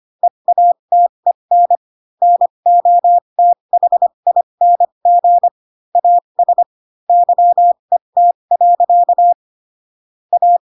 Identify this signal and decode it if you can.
Morse code, heard as a keyed tone, about 25 wpm, spaced as for 22 wpm, 700 Hz. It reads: EATEN NOTHING AS YET. A